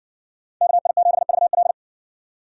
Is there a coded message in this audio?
Yes: Morse code from 0.6 to 1.7 s, heard as a keyed tone, about 60 wpm, 700 Hz. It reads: KI7VL